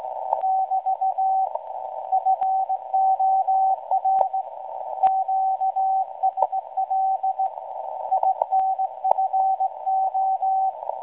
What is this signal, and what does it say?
Morse code, heard as a keyed tone, about 17 wpm, 760 Hz, with radio noise. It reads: <BT> FOR QSL INFO
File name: WebSDR records / SAQ at PI4THT msg 0800-0804 17.2kHz